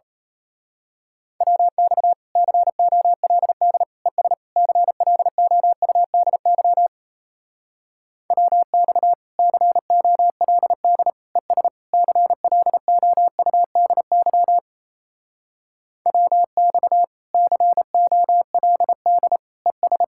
Morse code, heard as a keyed tone, about 32 wpm, 700 Hz. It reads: WX COLD ES CLOUDY WX COLD ES CLOUDY WX COLD ES
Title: Speed-Racing Common QSO Phrases - 28wpm